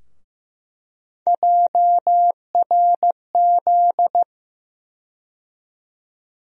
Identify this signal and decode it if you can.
Morse code, heard as a keyed tone, about 16 wpm, 700 Hz. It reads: JRZ
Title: USA Call Sign Postfixes - 3 Letters 15wpm